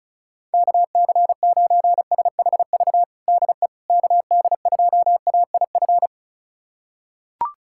Morse code, heard as a keyed tone, about 36 wpm, 700 Hz. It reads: KC9SHV DE KD2AIF
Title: USA Callsign DE Callsign - 6 Characters - 35wpm